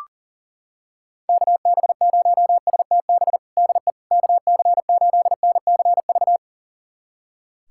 Morse code, heard as a keyed tone, about 37 wpm, 700 Hz. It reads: KB0STB DE KC8NCV
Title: USA Callsign DE Callsign - 6 Characters - 40wpm